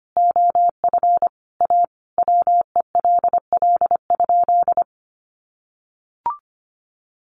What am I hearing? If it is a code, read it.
Morse code, heard as a keyed tone, about 25 wpm, 700 Hz. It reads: OF A WELL?
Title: Sentences from Top 900 Words 25wpm